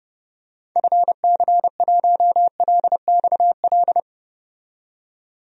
Morse code, heard as a keyed tone, about 30 wpm, 700 Hz. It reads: FC1LXL
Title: DXCC-100 Call Signs - 6 Characters - 30wpm